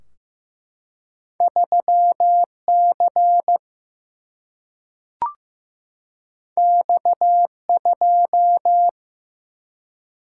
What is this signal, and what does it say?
Morse code, heard as a keyed tone, about 15 wpm, 700 Hz. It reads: 3 C X 2
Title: Single-Letter Single-Number 15wpm